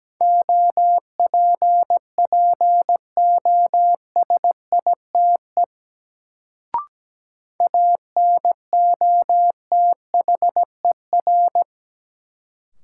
Morse code, heard as a keyed tone, about 17 wpm, 700 Hz. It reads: OPPOSITE ANOTHER
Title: Sets of 5 Words Encore - New Word Drill - 17wpm